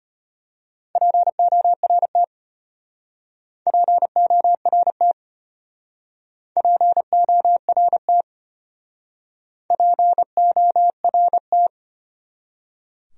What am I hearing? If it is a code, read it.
Morse code, heard as a keyed tone, about 32 wpm, 700 Hz. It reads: PORT PORT PORT PORT